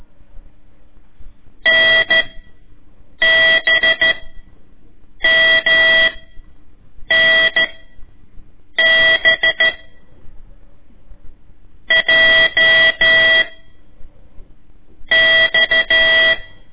Morse code, heard as a keyed tone, about 13 wpm, 1950 Hz. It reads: N B M N B J X
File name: ik1weg2 / CORSI CW / lez4-1